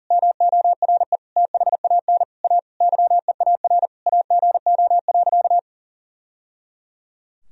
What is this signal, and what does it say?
Morse code, heard as a keyed tone, about 40 wpm, 700 Hz. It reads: MORE THAN A YEAR AGO.
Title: Sentences from Top 1000 Words - Review Part 3 - 40wpm